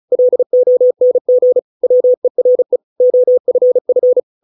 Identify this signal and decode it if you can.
Morse code, heard as a keyed tone, about 35 wpm, 500 Hz. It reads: LONG WERE OFF